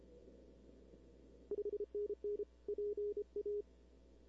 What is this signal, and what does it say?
Morse code, heard as a keyed tone, about 30 wpm, 410 Hz. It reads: 5NN PA
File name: arrl cw dx 2005